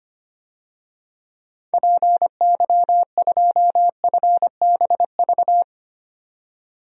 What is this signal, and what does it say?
Morse code, heard as a keyed tone, about 25 wpm, 690 Hz. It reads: PY2FBV
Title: DXCC-100 Call Signs - 6 Characters - 25wpm